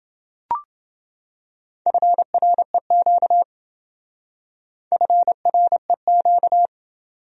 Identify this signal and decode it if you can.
Morse code, heard as a keyed tone, about 28 wpm, 700 Hz. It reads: FREQ FREQ